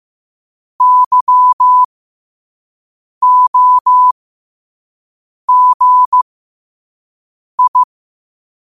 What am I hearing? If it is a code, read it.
Morse code, heard as a keyed tone, about 15 wpm, spaced as for 5 wpm, 1000 Hz. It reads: YOGI
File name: 5wpm